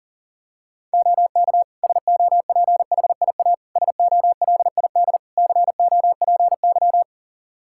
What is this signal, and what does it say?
Morse code, heard as a keyed tone, about 40 wpm, 700 Hz. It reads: OK SOPHIA SOLID COPY
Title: Practice QSO - Exchange 3 - More Info - 40wpm